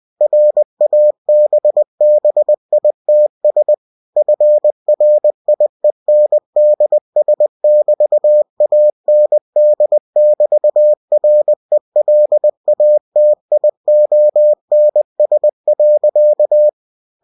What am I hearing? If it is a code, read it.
Morse code, heard as a keyed tone, about 20 wpm, 600 Hz. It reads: RABBITS FRIENDS<BT>AND<BT>RELATIONS.
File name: cw-books / winnie-the-pooh / winnie-20wpm-page-0079